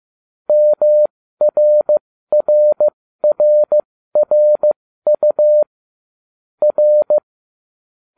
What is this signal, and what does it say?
Morse code, heard as a keyed tone, about 15 wpm, spaced as for 13 wpm, 600 Hz. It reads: MRRRRU R